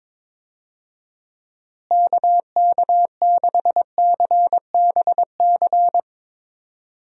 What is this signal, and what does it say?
Morse code, heard as a keyed tone, about 21 wpm, 700 Hz. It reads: KK6CBC
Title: USA Call Signs - 6 Characters - Repeated 3x - 22wpm